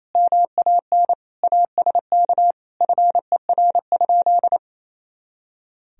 Morse code, heard as a keyed tone, about 27 wpm, 700 Hz. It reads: MAN ASK FER?
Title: Sentences from Top 200 Words and Abbreviations 28wpm